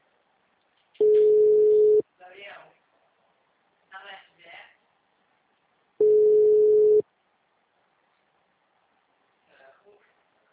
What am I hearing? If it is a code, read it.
Ringback tone (Europe, 425 Hz), with speech.